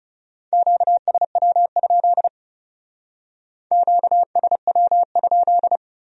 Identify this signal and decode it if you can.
Morse code, heard as a keyed tone, about 36 wpm, 700 Hz. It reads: QSW? QSW?
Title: Speed-Racing Q-Codes - 30wpm